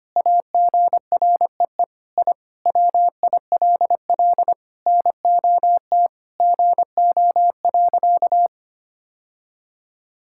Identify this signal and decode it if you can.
Morse code, heard as a keyed tone, about 25 wpm, 700 Hz. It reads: AGREE I WILL NOT GO.
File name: practice / 25wpm / Sentences with 4-Word Phrases - 25wpm